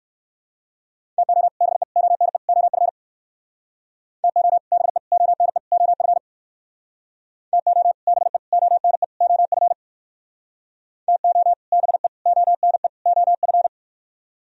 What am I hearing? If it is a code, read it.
Morse code, heard as a keyed tone, about 58 wpm, 700 Hz. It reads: TO BE ONE OF TO BE ONE OF TO BE ONE OF TO BE ONE OF